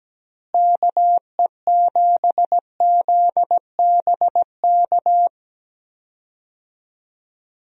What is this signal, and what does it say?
Morse code, heard as a keyed tone, about 17 wpm, 700 Hz. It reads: KE7ZBK
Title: USA Call Signs - 6 Characters - Repeated 3x - 17wpm